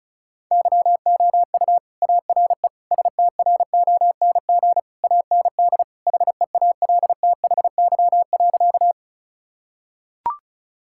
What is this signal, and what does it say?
Morse code, heard as a keyed tone, about 36 wpm, 700 Hz. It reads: YOU ARE STRONG AND HEALTHY.
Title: Sentences from Top 1000 Words - Review Part 5 - 35wpm